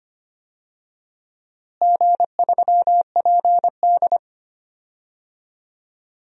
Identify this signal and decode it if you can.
Morse code, heard as a keyed tone, about 25 wpm, 700 Hz. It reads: G3PD
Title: DXCC-100 Call Signs - 4 Characters - Repeated 2x - 25wpm